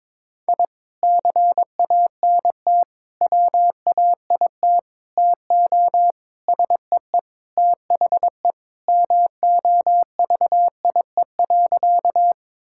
Morse code, heard as a keyed tone, about 22 wpm, 700 Hz. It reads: I CANT WAIT TO SEE THE MOVIE.
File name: practice / 22wpm / Sentences with 3-Word Phrases - 22wpm